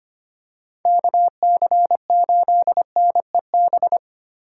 Morse code, heard as a keyed tone, about 29 wpm, 700 Hz. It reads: KC8NEB
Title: Speed Racing - USA 2x3 Call Signs 25wpm